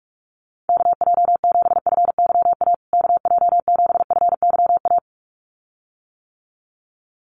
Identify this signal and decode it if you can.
Morse code, heard as a keyed tone, about 44 wpm, 700 Hz. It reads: KJ7FYA KJ7FYA
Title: USA Call Signs Repeated 2x 45wpm